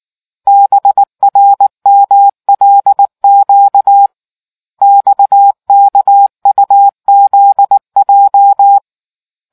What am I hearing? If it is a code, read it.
Morse code, heard as a keyed tone, about 19 wpm, 800 Hz. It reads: BRMLQ XKUZJ